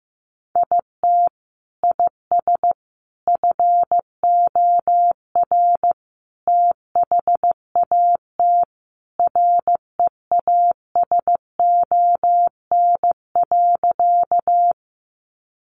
Morse code, heard as a keyed tone, about 15 wpm, 700 Hz. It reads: IT IS FOR THAT REASON.
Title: Sentences from Top 700 Words 15wpm